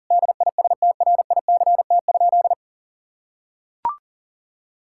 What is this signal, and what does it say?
Morse code, heard as a keyed tone, about 37 wpm, 700 Hz. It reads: DISTRICT?
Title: Sentences from Top 1600 Words 40wpm